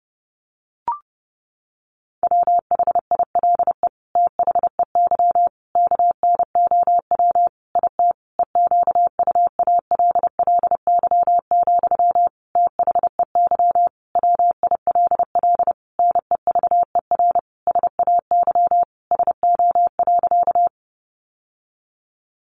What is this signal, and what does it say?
Morse code, heard as a keyed tone, about 30 wpm, 700 Hz. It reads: WHILE THEY KNOW IT EQUALLY, THEY WILL NEVER SAY SO.
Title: Sentences from Top 1400 Words 30wpm